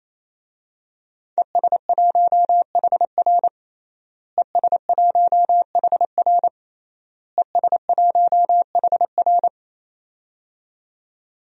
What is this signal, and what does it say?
Morse code, heard as a keyed tone, about 28 wpm, 700 Hz. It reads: ES1HR ES1HR ES1HR
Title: DXCC-100 Call Signs - 5 Characters - Repeated 3x - 28wpm